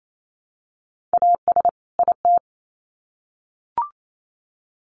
Morse code, heard as a keyed tone, about 28 wpm, 700 Hz. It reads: AS IT
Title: Sets of 2 Words 28wpm